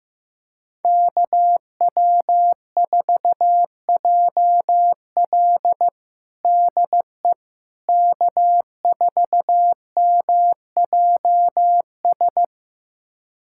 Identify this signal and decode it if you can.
Morse code, heard as a keyed tone, about 15 wpm, 700 Hz. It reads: KW4JL DE K4MJS
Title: USA Callsign DE Callsign - 5 Characters - 15wpm